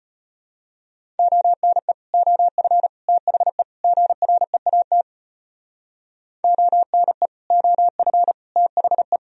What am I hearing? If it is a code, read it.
Morse code, heard as a keyed tone, about 32 wpm, 700 Hz. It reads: ONE OF THE GREAT ONE OF THE